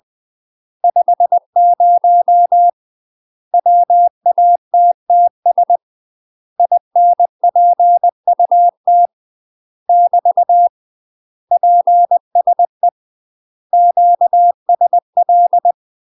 Morse code, heard as a keyed tone, about 20 wpm, 700 Hz. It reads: 50 WATTS INPUT <BT> PSE QSL
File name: UK MASTER QSOs TRAINING / 20WPM / UK MASTER QSOs 0015